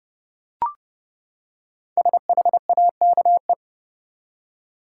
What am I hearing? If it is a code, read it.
Morse code, heard as a keyed tone, about 30 wpm, 700 Hz. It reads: SHAKE
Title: Five Letter Words - 30wpm